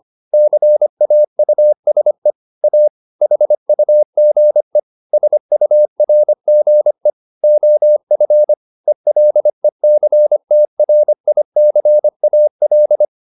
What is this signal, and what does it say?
Morse code, heard as a keyed tone, about 25 wpm, 600 Hz. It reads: CAUSE A HUGE SURGE OF ELECTRICAL